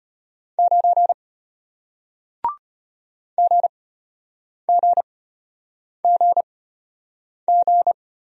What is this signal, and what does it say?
Morse code, heard as a keyed tone, about 34 wpm, spaced as for 8 wpm, 700 Hz. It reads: 9 GGGG